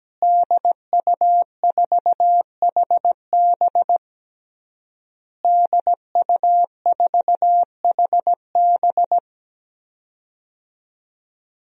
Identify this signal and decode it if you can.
Morse code, heard as a keyed tone, about 17 wpm, 700 Hz. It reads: DU4HB DU4HB
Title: DXCC-100 Call Signs - 5 Characters - Repeated 3x - 17wpm